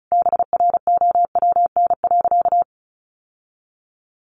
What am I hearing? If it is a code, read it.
Morse code, heard as a keyed tone, about 34 wpm, 700 Hz. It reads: BROWN.
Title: Sentences from Top 1200 Words 35wpm